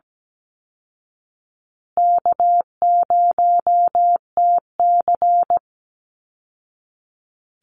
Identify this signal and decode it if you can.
Morse code, heard as a keyed tone, about 17 wpm, 700 Hz. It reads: K0TC
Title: Easy Call Signs Repeated 2x - 17wpm